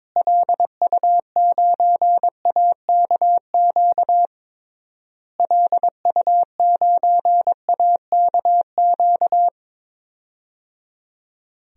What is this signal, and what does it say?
Morse code, heard as a keyed tone, about 22 wpm, 700 Hz. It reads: LU9AKQ LU9AKQ
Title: DXCC-100 Call Signs - 6 Characters - Repeated 3x - 22wpm